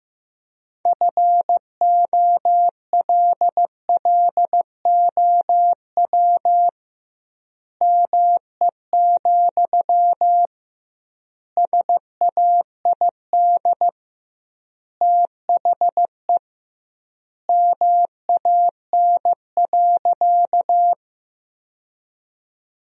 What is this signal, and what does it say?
Morse code, heard as a keyed tone, about 15 wpm, 700 Hz. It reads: FOLLOW ME, SAID THE MAN.
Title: 2x Word-Spacing - Sentences from Top 200 Words - 15wpm